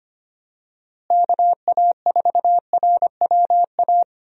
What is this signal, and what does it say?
Morse code, heard as a keyed tone, about 25 wpm, 700 Hz. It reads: KA4RWA
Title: USA Call Signs - 6 Characters - Repeated 2x - 25wpm